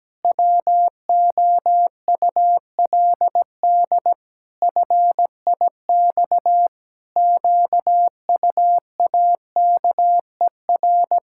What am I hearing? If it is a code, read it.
Morse code, heard as a keyed tone, about 17 wpm, 700 Hz. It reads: WOULD FIX QUAKER
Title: Pangram Challenge v2 - 17wpm